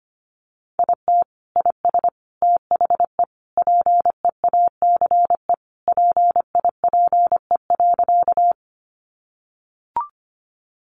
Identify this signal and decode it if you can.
Morse code, heard as a keyed tone, about 25 wpm, 700 Hz. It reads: IT IS THE PEACE PIPE.
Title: Sentences from Top 1500 Words 25wpm